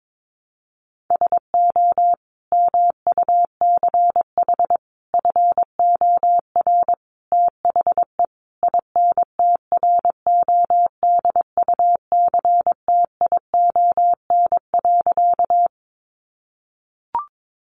Morse code, heard as a keyed tone, about 22 wpm, 700 Hz. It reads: SO MUCH FOR THE INTRODUCTION.
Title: Sentences from Top 1000 Words 22wpm